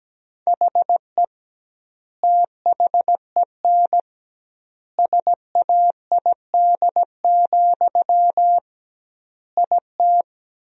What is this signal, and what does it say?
Morse code, heard as a keyed tone, about 17 wpm, 700 Hz. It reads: HE THEN SAID, IT
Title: 2x Word-Spacing - Sentences from Top 300 Words - 17wpm